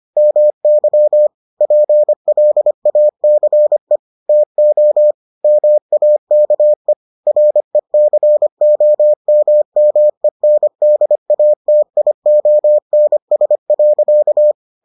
Morse code, heard as a keyed tone, about 25 wpm, 600 Hz. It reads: MY PLACE TO MAKE RECOMMENDATIONS.